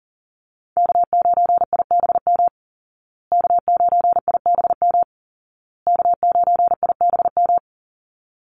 Morse code, heard as a keyed tone, about 40 wpm, 700 Hz. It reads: K9IBM K9IBM K9IBM
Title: Call Signs Repeated 2x - 40wpm